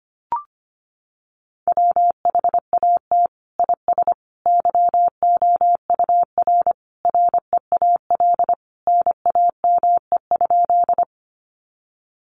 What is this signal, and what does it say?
Morse code, heard as a keyed tone, about 25 wpm, 700 Hz. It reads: WHAT IS YOUR REAL NAME?
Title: Sentences from Top 300 Words 25wpm